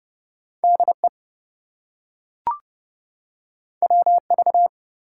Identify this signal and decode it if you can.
Morse code, heard as a keyed tone, about 30 wpm, 700 Hz. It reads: DE WV
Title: US States and Canadian Provinces 30wpm